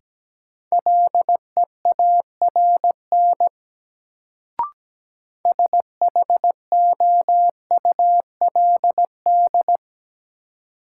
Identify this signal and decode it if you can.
Morse code, heard as a keyed tone, about 17 wpm, 700 Hz. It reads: LEARN SHOULD
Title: Sentences from Top 300 Words - New Word Drill - 17wpm